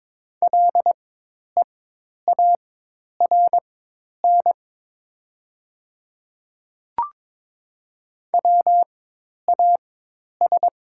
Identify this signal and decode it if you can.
Morse code, heard as a keyed tone, about 22 wpm, spaced as for 10 wpm, 700 Hz. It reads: LEARN WAS
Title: 4x Character Spacing - Top 300 Words - 22wpm